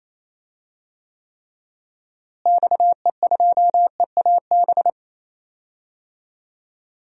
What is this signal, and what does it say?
Morse code, heard as a keyed tone, about 28 wpm, 700 Hz. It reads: XE2EAB
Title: DXCC-100 Call Signs - 6 Characters - Repeated 3x - 28wpm